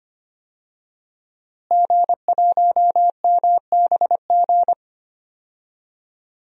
Morse code, heard as a keyed tone, about 25 wpm, 700 Hz. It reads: G1MBG
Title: DXCC-100 Call Signs - 5 Characters - Repeated 3x - 25wpm